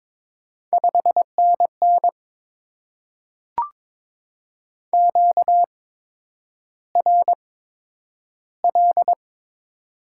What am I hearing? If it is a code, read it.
Morse code, heard as a keyed tone, about 22 wpm, spaced as for 9 wpm, 700 Hz. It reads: 5NN QRL
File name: Top 90 QSO Elements - 8x Character Spacing - 22wpm